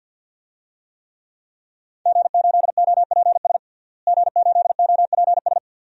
Morse code, heard as a keyed tone, about 52 wpm, 700 Hz. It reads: G8OPS G8OPS
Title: DXCC-100 Call Signs - 5 Characters - Repeated 2x - 50wpm